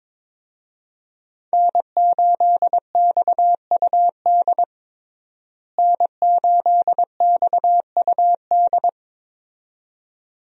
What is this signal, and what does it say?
Morse code, heard as a keyed tone, about 22 wpm, 700 Hz. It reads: N8XUD N8XUD
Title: USA Call Signs - 5 Characters - Repeated 2x - 22wpm